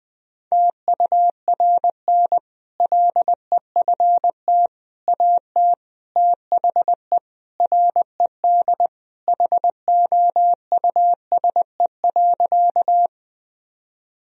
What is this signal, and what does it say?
Morse code, heard as a keyed tone, about 21 wpm, 700 Hz. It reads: TURN LEFT AT THE RED HOUSE.
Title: Sentences from Top 1000 Words - Review Part 2 - 20wpm